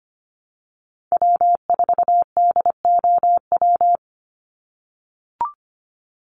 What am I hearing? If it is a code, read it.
Morse code, heard as a keyed tone, about 25 wpm, 700 Hz. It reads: W4DOW
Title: NA SOTA Chaser Callsigns 25wpm